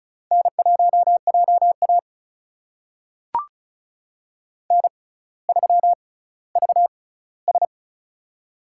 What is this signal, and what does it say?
Morse code, heard as a keyed tone, about 36 wpm, spaced as for 18 wpm, 700 Hz. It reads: N1JA N3VS
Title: USA Call Signs - 4 Characters - 6x Character Spacing - 35wpm